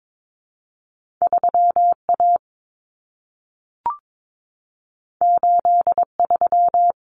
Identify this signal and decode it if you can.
Morse code, heard as a keyed tone, about 22 wpm, 700 Hz. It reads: 3A 83
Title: Common Cut Numbers - 2-Digit 22wpm